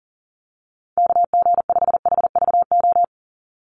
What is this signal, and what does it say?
Morse code, heard as a keyed tone, about 40 wpm, 700 Hz. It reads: KG5HVO
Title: Call Signs - 40wpm